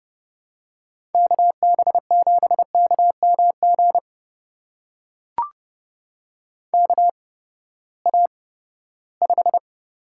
Morse code, heard as a keyed tone, about 30 wpm, spaced as for 17 wpm, 700 Hz. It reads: KB7KMG KA5
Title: USA Call Signs - 6 Characters - 8x Character Spacing - 30wpm